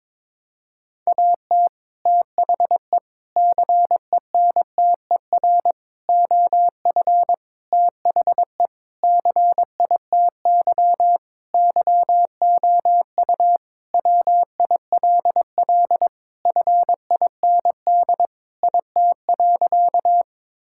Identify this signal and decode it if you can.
Morse code, heard as a keyed tone, about 22 wpm, 700 Hz. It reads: AT THE CENTER OF THE CITY YOU WILL FIND IT.
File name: practice / 22wpm / Sentences with 4-Word Phrases - 22wpm